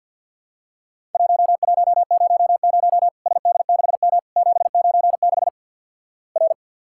Morse code, heard as a keyed tone, about 47 wpm, 700 Hz. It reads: 1100 SD6M 796 R
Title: Worked All Europe DX Contest - QTC Exchange with Repeats - 50wpm